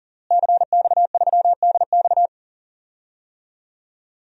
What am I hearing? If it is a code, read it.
Morse code, heard as a keyed tone, about 39 wpm, 700 Hz. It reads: CX3DX
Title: DXCC-100 Call Signs - 5 Characters - Repeated 3x - 40wpm